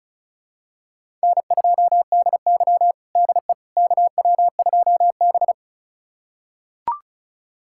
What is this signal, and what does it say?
Morse code, heard as a keyed tone, about 34 wpm, 700 Hz. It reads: N2DY DE KW2B
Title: USA Callsign DE Callsign - 4 Characters - 35wpm